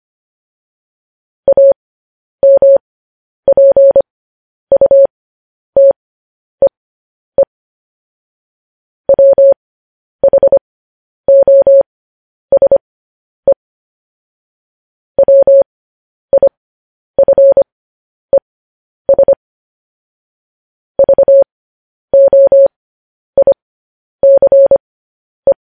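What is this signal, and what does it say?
Morse code, heard as a keyed tone, about 25 wpm, spaced as for 10 wpm, 550 Hz. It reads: AMPUTEE WHOSE WIFES VOICE
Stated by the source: Morse code news headlines transmission.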